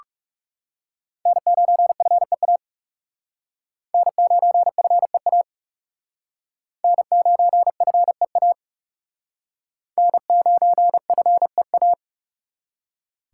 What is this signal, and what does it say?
Morse code, heard as a keyed tone, about 39 wpm, 700 Hz. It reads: N9FEA N9FEA N9FEA N9FEA